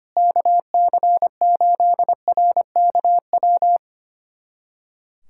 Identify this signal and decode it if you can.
Morse code, heard as a keyed tone, about 25 wpm, 700 Hz. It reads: KC8RKW
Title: USA Callsign DE Callsign - 6 Characters - 25wpm